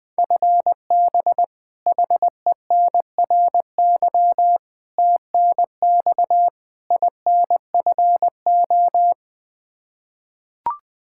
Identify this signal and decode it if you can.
Morse code, heard as a keyed tone, about 20 wpm, 700 Hz. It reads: FB HENRY TNX INFO
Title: Practice QSO - Exchange 3 - More Info - 20wpm